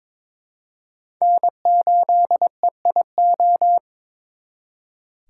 Morse code, heard as a keyed tone, about 26 wpm, 700 Hz. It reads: N8EIO
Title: Speed Racing - USA 1x3 Call Signs 22wpm